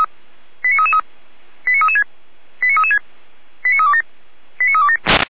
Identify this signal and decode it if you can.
Five-tone selective call (ZVEI), heard as quick runs of five tones, tones of about 70 ms. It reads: 89333 89336 89336 89327 89327
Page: /455/SDR/ZVEI/89330/